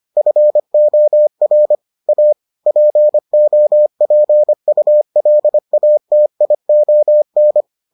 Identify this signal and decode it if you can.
Morse code, heard as a keyed tone, about 25 wpm, 600 Hz. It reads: FOR A POPULATION